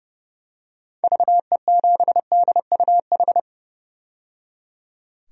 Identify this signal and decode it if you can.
Morse code, heard as a keyed tone, about 30 wpm, 700 Hz. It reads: VE7DUH
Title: DXCC-100 Call Signs - 6 Characters - 30wpm